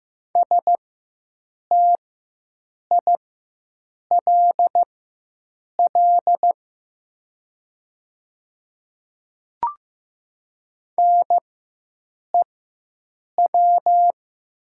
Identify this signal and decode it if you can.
Morse code, heard as a keyed tone, about 15 wpm, spaced as for 7 wpm, 700 Hz. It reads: STILL NEW
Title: 4x Character Spacing - Top 500 Words - 15wpm